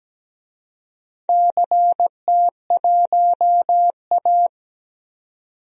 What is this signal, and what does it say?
Morse code, heard as a keyed tone, about 17 wpm, 700 Hz. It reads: CT1A
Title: DXCC-100 Call Signs - 4 Characters - 17wpm